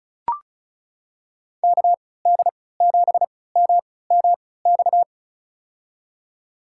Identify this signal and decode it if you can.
Morse code, heard as a keyed tone, about 36 wpm, spaced as for 20 wpm, 700 Hz. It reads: KD7MMX
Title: USA Call Signs - 6 Characters - 3x Character Spacing - 35wpm